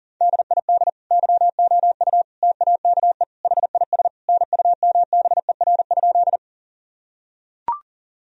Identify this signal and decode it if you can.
Morse code, heard as a keyed tone, about 40 wpm, 700 Hz. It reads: DID YOU TAKE HIS NUMBER?
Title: Sentences from Top 1000 Words - Review Part 1 - 40wpm